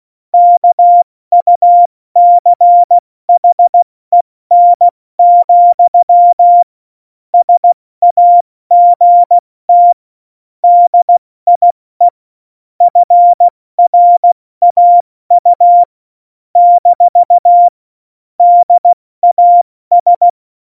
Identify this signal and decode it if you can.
Morse code, heard as a keyed tone, about 16 wpm, 700 Hz. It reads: KUCHEN, SAGT DIE FRAU - DAS
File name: Morsekurs / Witze / Witz 23. April 2019, 70 F80